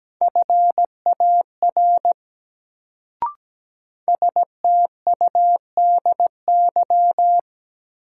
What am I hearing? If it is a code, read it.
Morse code, heard as a keyed tone, about 17 wpm, 700 Hz. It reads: FAR STUDY
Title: Sentences from Top 300 Words - New Word Drill - 17wpm